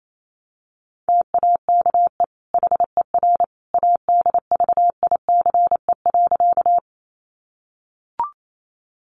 Morse code, heard as a keyed tone, about 28 wpm, 700 Hz. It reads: TAKE HER ADVICE.
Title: Sentences from Top 1100 Words 28wpm